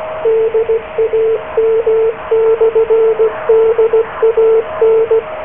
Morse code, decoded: DAM/DAN